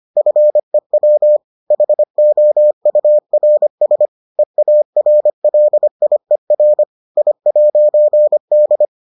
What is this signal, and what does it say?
Morse code, heard as a keyed tone, about 25 wpm, 600 Hz. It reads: FEW HOURS EARLIER I'D